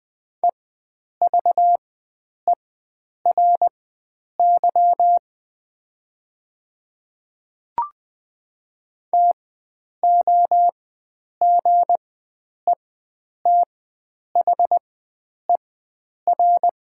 Morse code, heard as a keyed tone, about 20 wpm, spaced as for 9 wpm, 700 Hz. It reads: EVERY TOGETHER
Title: 4x Character Spacing - Top 300 Words - 20wpm